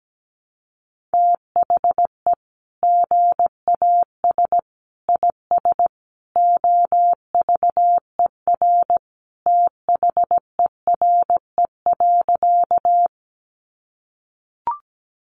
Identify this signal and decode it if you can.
Morse code, heard as a keyed tone, about 17 wpm, 700 Hz. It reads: THE GAS IS OVER THERE.
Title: Sentences from Top 1100 Words 17wpm